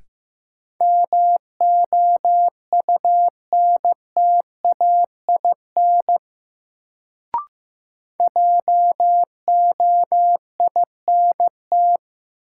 Morse code, heard as a keyed tone, about 15 wpm, 700 Hz. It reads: MOUNTAIN JOINT
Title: Sentences from Top 1300 Words - New Word Drill Alt - 15wpm